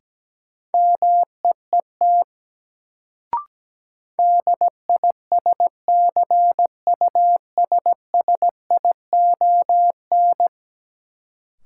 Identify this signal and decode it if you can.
Morse code, heard as a keyed tone, about 17 wpm, 700 Hz. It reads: MEET DISCUSSION
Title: Sentences from Top 1100 Words - New Word Drill Alt - 17wpm